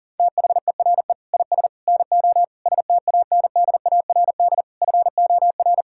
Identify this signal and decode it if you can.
Morse code, heard as a keyed tone, about 40 wpm, 700 Hz. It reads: THERE IS NO STANDARD FOR
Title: Sentences from Top 1600 Words 40wpm